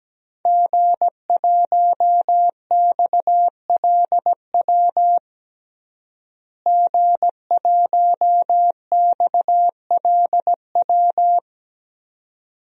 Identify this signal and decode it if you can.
Morse code, heard as a keyed tone, about 17 wpm, 700 Hz. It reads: G1XLW G1XLW
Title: DXCC-100 Call Signs - 5 Characters - Repeated 2x - 17wpm